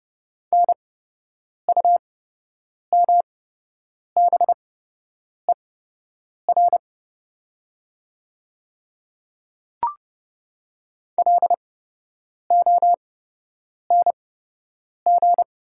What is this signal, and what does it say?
Morse code, heard as a keyed tone, about 30 wpm, spaced as for 8 wpm, 690 Hz. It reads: NUMBER LONG
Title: 8x Character Spacing - Top 200 Words - 30wpm